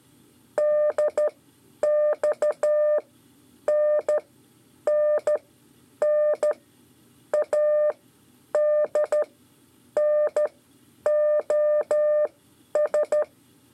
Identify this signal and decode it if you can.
Morse code, heard as a keyed tone, about 12 wpm, spaced as for 8 wpm, 600 Hz. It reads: DXNNNADNOS
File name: 5 LEZIONE RIPASSO 1